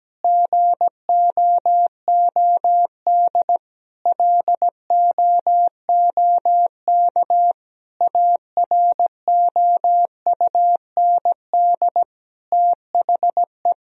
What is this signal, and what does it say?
Morse code, heard as a keyed tone, about 17 wpm, 700 Hz. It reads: GOOD LOOK AROUND THE
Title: Sentences from Top 1000 Words - Review Part 3 - 17wpm